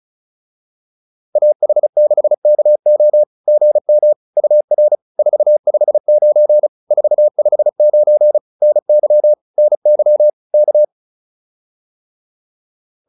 Morse code, heard as a keyed tone, about 34 wpm, 600 Hz. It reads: AH6KO GM UR 459 459 NY NY K